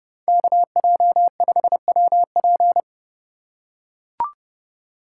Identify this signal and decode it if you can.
Morse code, heard as a keyed tone, about 30 wpm, 700 Hz. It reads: KJ5WP
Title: USA Call Signs - 5 Characters - 4x Character Spacing - 30wpm